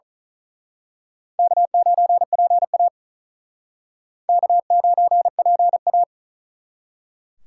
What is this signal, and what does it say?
Morse code, heard as a keyed tone, about 39 wpm, 700 Hz. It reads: K9PA K9PA
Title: Speed Racing - USA 1x2 Call Signs 35wpm